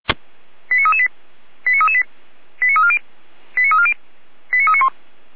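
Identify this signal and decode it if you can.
Five-tone selective call (ZVEI), heard as quick runs of five tones, tones of about 70 ms. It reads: 89338 89338 89340 89340 89391
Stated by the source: ZVEI five-tone selective call 89319 received with an SDR, hosted by Feuerwehr Orlamünde (fire brigade).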